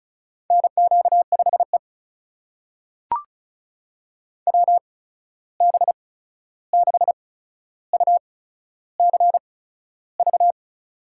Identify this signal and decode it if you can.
Morse code, heard as a keyed tone, about 36 wpm, spaced as for 13 wpm, 700 Hz. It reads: NQ5E WB6UCV
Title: USA Call Signs - 8x Character Spacing - 35wpm